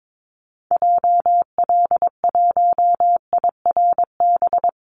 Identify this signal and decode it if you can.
Morse code, heard as a keyed tone, about 22 wpm, 700 Hz. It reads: JL1IRB
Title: Call Signs Repeated 3x - 22wpm